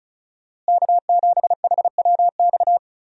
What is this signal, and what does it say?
Morse code, heard as a keyed tone, about 36 wpm, 700 Hz. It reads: K7HWX
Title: USA Call Signs - 5 Characters - Repeated 3x - 35wpm